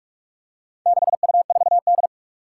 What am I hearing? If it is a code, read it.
Morse code, heard as a keyed tone, about 42 wpm, 700 Hz. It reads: 6U4D
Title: Four Challenging Letter-Number Combinations 45wpm